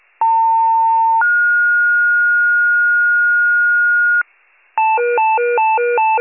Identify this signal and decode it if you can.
Two-tone page, heard as a one-second tone, then a three-second tone, about 900 then 1430 Hz, with noise. It is heard from 0.2 to 4.2 s.